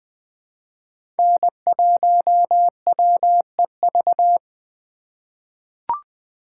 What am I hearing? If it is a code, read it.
Morse code, heard as a keyed tone, about 20 wpm, 700 Hz. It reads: N1WEV
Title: USA Call Signs - 5 Characters - 2x Character Spacing - 20wpm